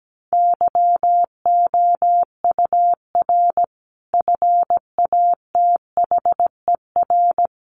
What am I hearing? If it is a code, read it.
Morse code, heard as a keyed tone, about 17 wpm, 700 Hz. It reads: YOUR FATHER
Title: Sentences from Top 1400 Words 17wpm